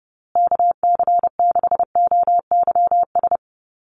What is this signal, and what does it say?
Morse code, heard as a keyed tone, about 30 wpm, 700 Hz. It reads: KC6OYS
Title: USA Call Signs Repeated 2x 30wpm